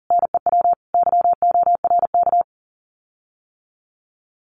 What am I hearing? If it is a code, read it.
Morse code, heard as a keyed tone, about 40 wpm, 700 Hz. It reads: NEW YORK
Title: Common QSO Phrases Encore 40wpm